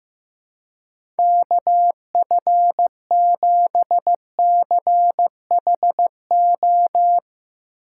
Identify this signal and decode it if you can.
Morse code, heard as a keyed tone, about 15 wpm, 700 Hz. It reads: KF7CHO
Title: USA Call Signs - 6 Characters - 4x Character Spacing - 15wpm